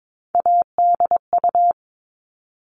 Morse code, heard as a keyed tone, about 22 wpm, 700 Hz. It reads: ADU
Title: All Cut Numbers - 3-Digit 22wpm